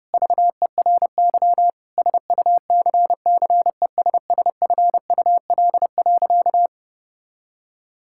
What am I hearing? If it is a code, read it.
Morse code, heard as a keyed tone, about 30 wpm, 700 Hz. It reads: VERY SUCCESSFUL.
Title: Sentences from Top 1000 Words - Review Part 6 - 30wpm